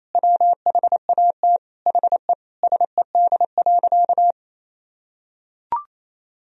Morse code, heard as a keyed tone, about 28 wpm, 700 Hz. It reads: WHAT HE SED.
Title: Sentences from Top 400 Words and Abbreviations 28wpm